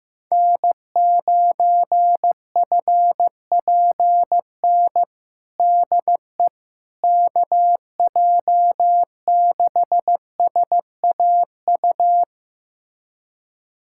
Morse code, heard as a keyed tone, about 15 wpm, 700 Hz. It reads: N9FPN DE KJ6SAU
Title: USA Callsign DE Callsign 15wpm